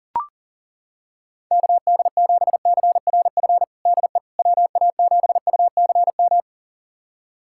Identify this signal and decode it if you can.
Morse code, heard as a keyed tone, about 40 wpm, 700 Hz. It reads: KD7CRF DE WA7UCM
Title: USA Callsign DE Callsign - 6 Characters - 40wpm